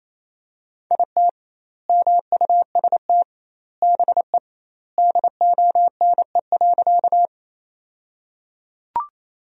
Morse code, heard as a keyed tone, about 28 wpm, 700 Hz. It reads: IT MUST BE DONE.
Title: 2x Word-Spacing - Sentences from Top 500 Words - 28wpm